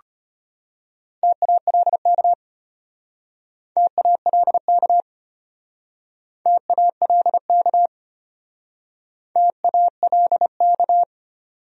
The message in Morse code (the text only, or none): TALK TALK TALK TALK